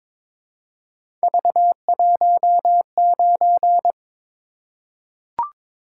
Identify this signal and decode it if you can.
Morse code, heard as a keyed tone, about 22 wpm, 700 Hz. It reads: V19
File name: practice / 22wpm / Three Letter-Number Combinations - 22wpm